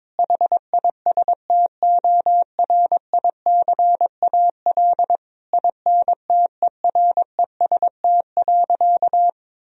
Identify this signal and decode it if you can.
Morse code, heard as a keyed tone, about 22 wpm, 700 Hz. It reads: HISTORICAL INTEREST.
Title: Sentences from Top 1000 Words - Review Part 4 - 22wpm